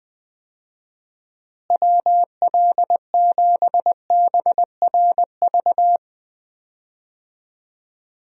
Morse code, heard as a keyed tone, about 20 wpm, 700 Hz. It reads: WL7BRV
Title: USA Call Signs - 6 Characters - Repeated 2x - 20wpm